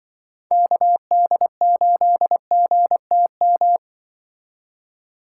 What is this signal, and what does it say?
Morse code, heard as a keyed tone, about 24 wpm, 700 Hz. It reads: KD8GTM
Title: Speed Racing - USA 2x3 Call Signs 20wpm